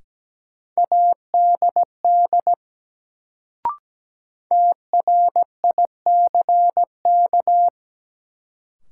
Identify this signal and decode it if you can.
Morse code, heard as a keyed tone, about 17 wpm, 700 Hz. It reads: ADD TRICK
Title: Sentences from Top 1500 Words - New Word Drill Alt - 17wpm